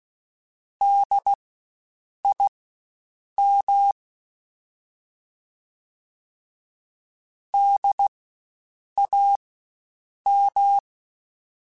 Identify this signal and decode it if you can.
Morse code, heard as a keyed tone, about 16 wpm, spaced as for 7 wpm, 800 Hz. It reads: DIM DAM
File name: Spaced Alphas / Spaced Alphas mp3s / Ds sp @15